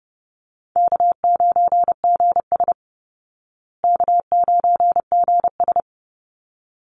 Morse code, heard as a keyed tone, about 30 wpm, 700 Hz. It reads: K9GS K9GS
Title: Call Signs Repeated 3x - 30wpm